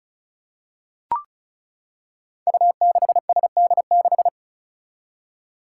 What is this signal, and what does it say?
Morse code, heard as a keyed tone, about 36 wpm, 700 Hz. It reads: U6SD6
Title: Five Challenging Letter-Number Combinations 35wpm